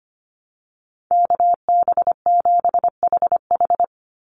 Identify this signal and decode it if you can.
Morse code, heard as a keyed tone, about 25 wpm, 700 Hz. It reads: KB7HH
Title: NA SOTA Chaser Callsigns 25wpm